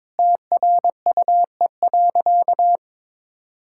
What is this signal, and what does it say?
Morse code, heard as a keyed tone, about 22 wpm, 700 Hz. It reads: TRUE.
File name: practice / 22wpm / Sentences with 4-Word Phrases - 22wpm